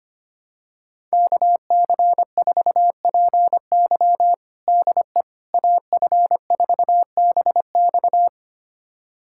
Morse code, heard as a keyed tone, about 25 wpm, 700 Hz. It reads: KC4PY DE AF4BX
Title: USA Callsign DE Callsign - 5 Characters - 25wpm